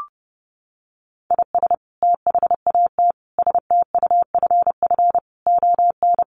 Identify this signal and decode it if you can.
Morse code, heard as a keyed tone, about 30 wpm, 700 Hz. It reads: IS THAT STUFF ON
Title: Sentences from Top 1400 Words 30wpm